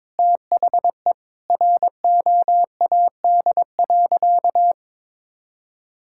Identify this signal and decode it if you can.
Morse code, heard as a keyed tone, about 22 wpm, 700 Hz. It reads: THE ROAD.
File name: practice / 22wpm / Sentences with 4-Word Phrases - 22wpm